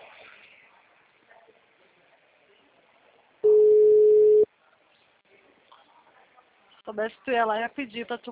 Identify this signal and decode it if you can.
Ringback tone (Europe, 425 Hz), with speech.